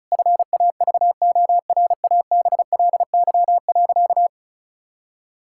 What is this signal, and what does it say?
Morse code, heard as a keyed tone, about 36 wpm, 700 Hz. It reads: FAVORABLY.
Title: Pangram Challenge v2 - 35wpm